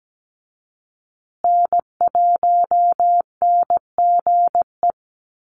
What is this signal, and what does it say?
Morse code, heard as a keyed tone, about 17 wpm, 700 Hz. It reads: N1NGE
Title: USA Call Signs Repeated 3x 17wpm